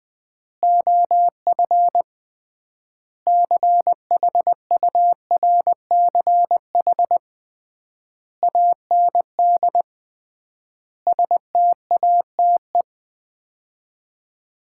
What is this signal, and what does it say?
Morse code, heard as a keyed tone, about 20 wpm, 700 Hz. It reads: OF CHURCH AND STATE
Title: 3x Word-Spacing - Sets of 5 Words - 20wpm